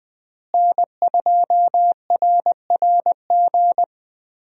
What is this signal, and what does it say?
Morse code, heard as a keyed tone, about 20 wpm, 700 Hz. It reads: N2RRG
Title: USA Call Signs - 5 Characters - Repeated 3x - 20wpm